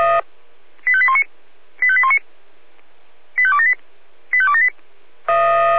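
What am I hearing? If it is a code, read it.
Five-tone selective call (ZVEI), heard as quick runs of five tones, tones of about 70 ms. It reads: 85819 85819 85278 85278
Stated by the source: Five-tone ZVEI selective call received on an SDR.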